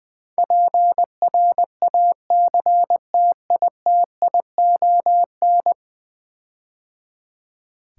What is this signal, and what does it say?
Morse code, heard as a keyed tone, about 20 wpm, 700 Hz. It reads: PRACTITION
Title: Build A Word - 5 Letters At A Time - 12-Letter Words - 20wpm